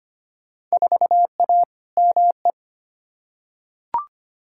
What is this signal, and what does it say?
Morse code, heard as a keyed tone, about 25 wpm, 700 Hz. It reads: 4A ME
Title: Field Day Exchanges - 25wpm